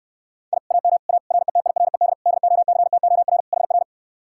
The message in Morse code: I AM A RETIRED DOCTOR HR

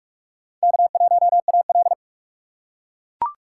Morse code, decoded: K1AL

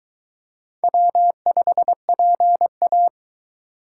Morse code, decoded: W5PA

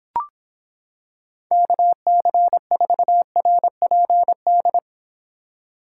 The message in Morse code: KC4RPD

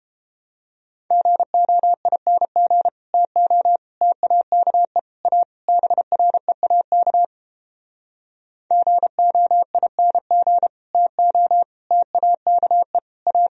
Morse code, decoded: GOING TO TAKE A BREAK GOING TO TAKE A

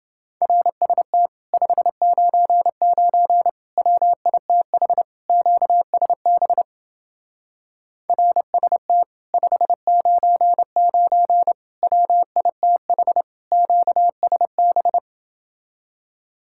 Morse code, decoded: RST 599 WITH QSB RST 599 WITH QSB

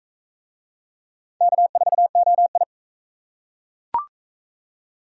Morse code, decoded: K4OI

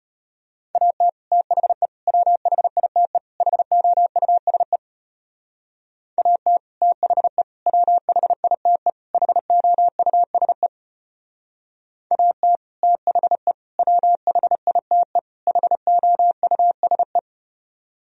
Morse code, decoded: AT THE WHITE HOUSE AT THE WHITE HOUSE AT THE WHITE HOUSE

